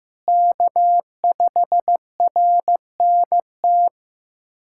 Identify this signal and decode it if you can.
Morse code, heard as a keyed tone, about 16 wpm, 700 Hz. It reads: K5RNT
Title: USA Callsign DE Callsign - 5 Characters - 15wpm